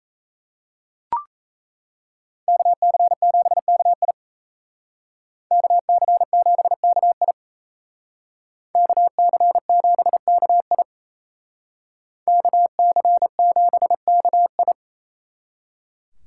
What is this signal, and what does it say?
Morse code, heard as a keyed tone, about 37 wpm, 700 Hz. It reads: KC7KI KC7KI KC7KI KC7KI